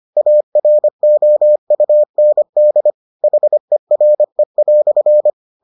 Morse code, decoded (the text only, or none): AROUND HERE"